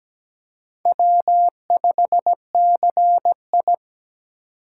Morse code, heard as a keyed tone, about 20 wpm, 700 Hz. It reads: W5CI
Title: Speed Racing - USA 1x2 Call Signs 17wpm